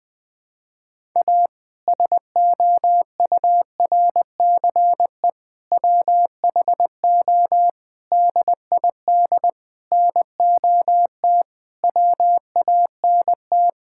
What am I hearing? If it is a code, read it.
Morse code, heard as a keyed tone, about 20 wpm, 700 Hz. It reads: A SOURCE WHO DID NOT WANT
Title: Sentences with 8-Word Phrases - 20wpm